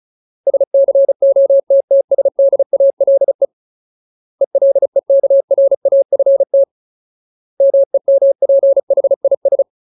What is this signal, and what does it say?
Morse code, heard as a keyed tone, about 34 wpm, 550 Hz. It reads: SCOTTSDALE ELEKRAFT MEMPHIS
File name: Ultimate Word List / 35WPM Ultimate